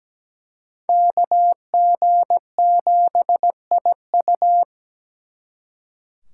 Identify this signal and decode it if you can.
Morse code, heard as a keyed tone, about 20 wpm, 700 Hz. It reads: KG7IU
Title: Speed Racing - USA 2x2 Call Signs 17wpm